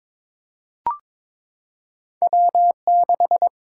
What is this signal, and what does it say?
Morse code, heard as a keyed tone, about 22 wpm, 700 Hz. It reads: W6
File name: Two Letter-Number Combinations - 22wpm